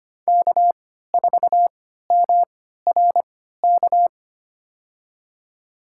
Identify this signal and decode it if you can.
Morse code, heard as a keyed tone, about 25 wpm, spaced as for 14 wpm, 700 Hz. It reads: K4MRK
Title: USA Call Signs - 5 Characters - 3x Character Spacing - 25wpm